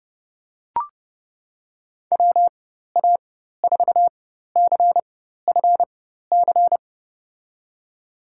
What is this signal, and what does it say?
Morse code, heard as a keyed tone, about 30 wpm, spaced as for 14 wpm, 700 Hz. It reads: WA4CFC
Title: USA Call Signs - 6 Characters - 4x Character Spacing - 30wpm